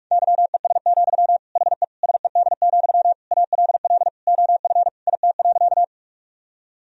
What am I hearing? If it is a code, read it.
Morse code, heard as a keyed tone, about 46 wpm, 700 Hz. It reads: YES, HE SED, ALL OF IT.
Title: Sentences from Top 500 Words and Abbreviations 45wpm